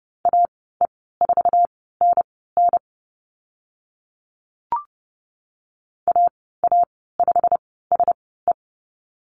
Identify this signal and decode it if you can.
Morse code, heard as a keyed tone, about 30 wpm, spaced as for 17 wpm, 700 Hz. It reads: AE4NN AA5SE
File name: Lesson 15g - Number 5 - Call Signs 30wpm